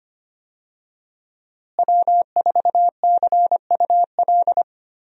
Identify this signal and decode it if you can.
Morse code, heard as a keyed tone, about 25 wpm, 700 Hz. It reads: W4CUL
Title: USA Call Signs - 5 Characters - Repeated 2x - 25wpm